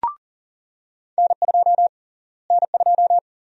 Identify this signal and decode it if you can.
Morse code, heard as a keyed tone, about 40 wpm, 700 Hz. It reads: N2 N2
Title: USA Call Sign Prefixes - Repeated 2x - 40wpm